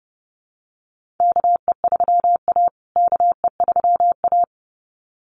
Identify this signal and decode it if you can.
Morse code, heard as a keyed tone, about 30 wpm, 700 Hz. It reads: KE3A KE3A
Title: USA Call Signs Repeated 2x 30wpm